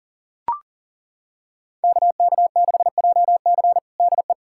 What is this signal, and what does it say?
Morse code, heard as a keyed tone, about 40 wpm, 700 Hz. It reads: KK6JC DE
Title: USA Callsign DE Callsign - 5 Characters - 40wpm